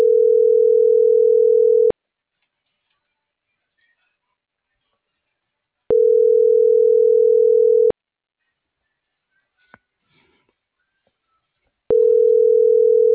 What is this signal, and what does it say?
Ringback tone (North America), 440 + 480 Hz.